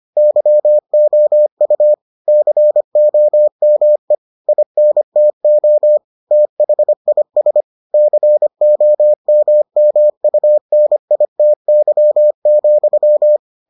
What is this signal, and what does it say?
Morse code, heard as a keyed tone, about 25 wpm, 600 Hz. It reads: YOU COME INTO THIS COMMUNITY,